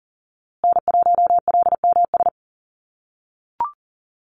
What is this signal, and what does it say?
Morse code, heard as a keyed tone, about 40 wpm, 700 Hz. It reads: N1LMS
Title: USA Call Signs Repeated 2x 40wpm